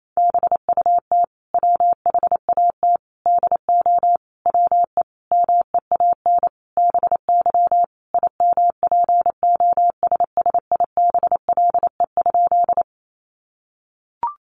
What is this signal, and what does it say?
Morse code, heard as a keyed tone, about 28 wpm, 700 Hz. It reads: BUT WHAT DO WE MEAN BY IMPOSSIBLE?
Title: Sentences from Top 1100 Words 28wpm